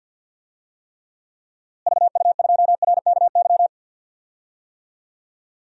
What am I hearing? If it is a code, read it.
Morse code, heard as a keyed tone, about 51 wpm, 700 Hz. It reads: VU2RKY
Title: DXCC-100 Call Signs - 6 Characters - Repeated 3x - 50wpm